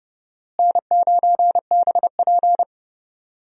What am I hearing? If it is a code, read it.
Morse code, heard as a keyed tone, about 30 wpm, 700 Hz. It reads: N9BP
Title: USA Call Signs - 4 Characters - Repeated 2x - 30wpm